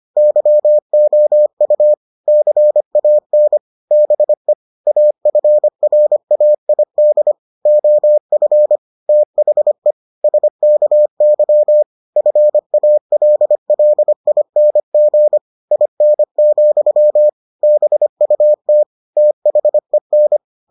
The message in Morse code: YOU CAN BE AFRAID OF THE SKY FALLING IN, BUT THEN

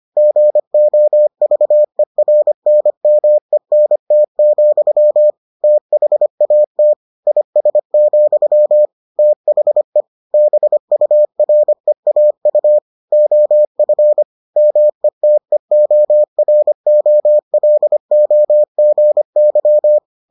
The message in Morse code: GOVERNMENT, THAT IS, THE BUREAU OF METEOROLOGY

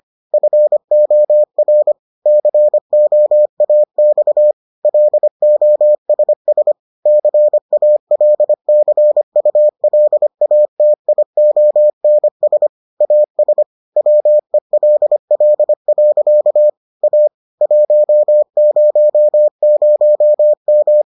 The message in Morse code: FOR COAX LOSS CALCULATIONS AS WELL. A 100M

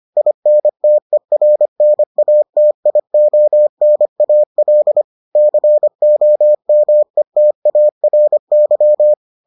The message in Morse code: INTERNATIONAL COMETARY

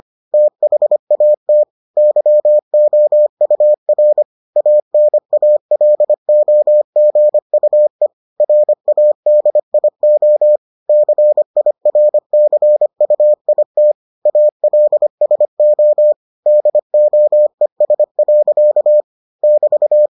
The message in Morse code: THAT YOUR ANALOGUE RADIO CIRCUIT ALSO DOES. <BT>

